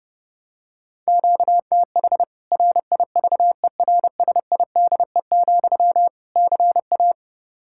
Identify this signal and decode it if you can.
Morse code, heard as a keyed tone, about 32 wpm, 700 Hz. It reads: QTH RIVERSIDE, CA
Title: K1USN SST QSO Practice 30wpm